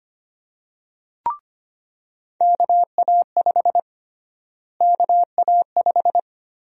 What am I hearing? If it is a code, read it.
Morse code, heard as a keyed tone, about 25 wpm, 700 Hz. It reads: KA5 KA5
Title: USA Call Sign Prefixes - Repeated 2x - 25wpm